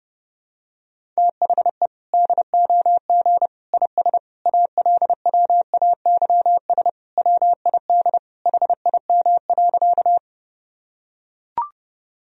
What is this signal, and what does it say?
Morse code, heard as a keyed tone, about 30 wpm, 700 Hz. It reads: THE DOG IS ALWAYS WID HIM.
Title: Sentences from Top 400 Words and Abbreviations 30wpm